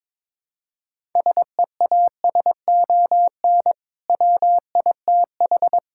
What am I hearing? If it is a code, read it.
Morse code, heard as a keyed tone, about 22 wpm, 700 Hz. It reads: SEASON WITH
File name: practice / 22wpm / Build A Phrase - 5-Word Phrases - 22wpm